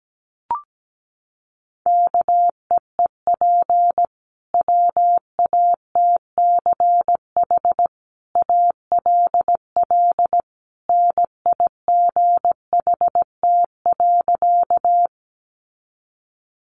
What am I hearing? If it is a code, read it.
Morse code, heard as a keyed tone, about 17 wpm, 700 Hz. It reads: KEEP WATCH ALL NIGHT.
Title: Sentences from Top 300 Words 17wpm